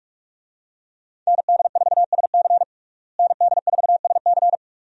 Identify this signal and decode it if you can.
Morse code, heard as a keyed tone, about 46 wpm, 700 Hz. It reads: ND4SC ND4SC
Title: USA Call Signs - 5 Characters - Repeated 2x - 45wpm